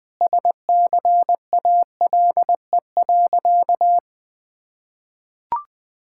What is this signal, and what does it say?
Morse code, heard as a keyed tone, about 20 wpm, 700 Hz. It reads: SCALE.
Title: Sentences from Top 1000 Words - Review Part 5 - 20wpm